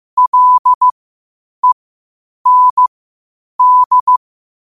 Morse code, heard as a keyed tone, about 15 wpm, spaced as for 8 wpm, 1000 Hz. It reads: LEND
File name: qotdfiles / 8wpm